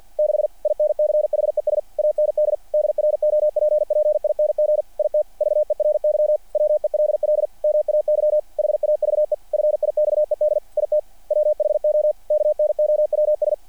Morse code, heard as a keyed tone, about 48 wpm, 600 Hz. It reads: 6 INCHES AND DROPPING IT VERY WELL MAY HAVE FIXED IT WHO KNOWS
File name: archived / CRSnet